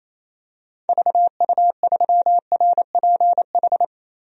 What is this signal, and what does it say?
Morse code, heard as a keyed tone, about 27 wpm, 700 Hz. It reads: VU3RPH
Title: DXCC-100 Call Signs - 6 Characters - 28wpm